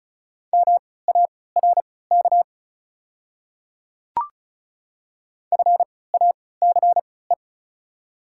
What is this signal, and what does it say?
Morse code, heard as a keyed tone, about 34 wpm, spaced as for 20 wpm, 690 Hz. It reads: MARK FACE
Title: 3x Character Spacing - Top 500 Words - 35wpm